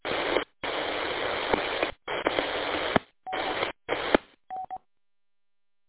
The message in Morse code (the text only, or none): none